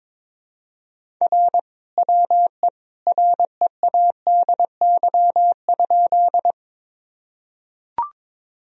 Morse code, heard as a keyed tone, about 22 wpm, 700 Hz. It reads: R WE READY?
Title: Sentences from Top 400 Words and Abbreviations 22wpm